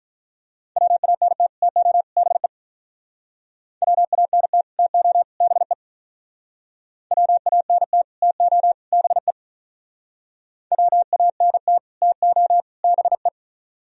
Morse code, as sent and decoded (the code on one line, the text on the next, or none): .-- .- -. - / - --- / -... / .-- .- -. - / - --- / -... / .-- .- -. - / - --- / -... / .-- .- -. - / - --- / -... .
WANT TO BE WANT TO BE WANT TO BE WANT TO BE